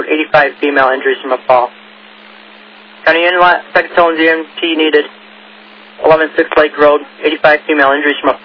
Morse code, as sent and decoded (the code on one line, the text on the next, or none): none